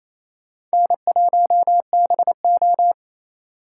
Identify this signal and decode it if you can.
Morse code, heard as a keyed tone, about 38 wpm, spaced as for 33 wpm, 700 Hz. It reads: N1BO
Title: Speed Racing - USA 1x2 Call Signs 28wpm